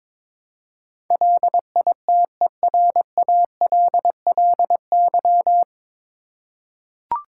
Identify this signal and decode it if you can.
Morse code, heard as a keyed tone, about 22 wpm, 700 Hz. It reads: LITERALLY
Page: Morse Code Ninja practice 22wpm